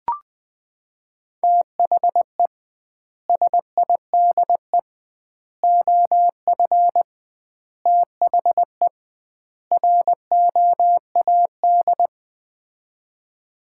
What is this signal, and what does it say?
Morse code, heard as a keyed tone, about 20 wpm, 700 Hz. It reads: THE SIDE OF THE ROAD
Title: 2x Word-Spacing - Sets of 5 Words - 20wpm